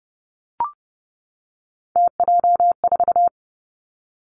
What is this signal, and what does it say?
Morse code, heard as a keyed tone, about 31 wpm, 700 Hz. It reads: TJ4
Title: Three Challenging Letter-Number Combinations 30wpm